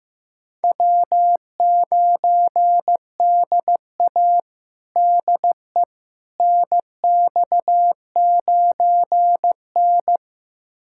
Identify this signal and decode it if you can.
Morse code, heard as a keyed tone, about 15 wpm, 700 Hz. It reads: W9DA DE NX9N
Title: USA Callsign DE Callsign - 4 Characters - 15wpm